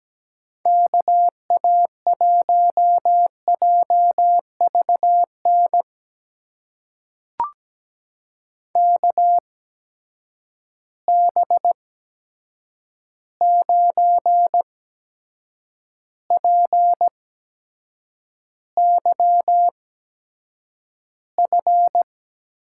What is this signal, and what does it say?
Morse code, heard as a keyed tone, about 17 wpm, spaced as for 7 wpm, 700 Hz. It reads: KA1JVN KB9PYF
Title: USA Call Signs - 6 Characters - 8x Character Spacing - 17wpm